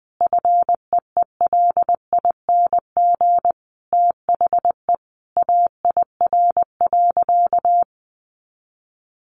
Morse code, heard as a keyed tone, about 20 wpm, 700 Hz. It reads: FEELING THE AIR.
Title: A Sound Of Thunder 20wpm